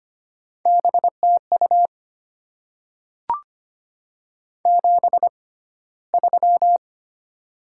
Morse code, heard as a keyed tone, about 25 wpm, spaced as for 15 wpm, 690 Hz. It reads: BTU 73
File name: Top 20 QSO Elements - 6x Character Spacing - 25wpm